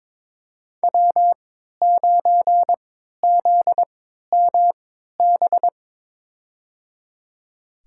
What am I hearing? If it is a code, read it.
Morse code, heard as a keyed tone, about 23 wpm, spaced as for 13 wpm, 700 Hz. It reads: W9ZMB
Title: USA Call Signs - 5 Characters - 3x Character Spacing - 22wpm